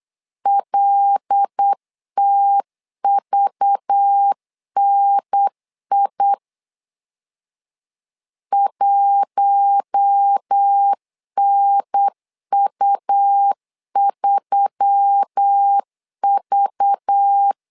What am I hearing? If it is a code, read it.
Morse code, heard as a keyed tone, about 8 wpm, 800 Hz. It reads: LTVNI 1NU3V
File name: LEZIONI / 5 LBVC / 5 LBV 8wpm 2837